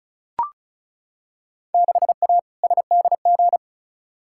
Morse code, heard as a keyed tone, about 36 wpm, 700 Hz. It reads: 6A SDG
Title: Field Day Exchanges Encore - 35wpm